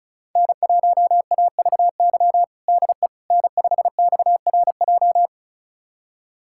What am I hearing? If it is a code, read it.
Morse code, heard as a keyed tone, about 36 wpm, 700 Hz. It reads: N1AVY DE N5XRJ
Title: USA Callsign DE Callsign - 5 Characters - 35wpm